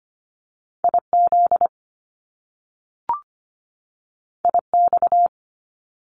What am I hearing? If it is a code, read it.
Morse code, heard as a keyed tone, about 25 wpm, 700 Hz. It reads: IZ IX
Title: Top Two Letter Combinations in Words 25wpm